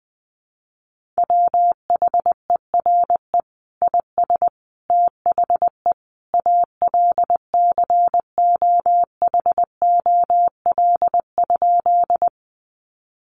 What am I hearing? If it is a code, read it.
Morse code, heard as a keyed tone, about 20 wpm, 700 Hz. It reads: WHERE IS THE ALCOHOL?
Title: Sentences from Top 1400 Words 20wpm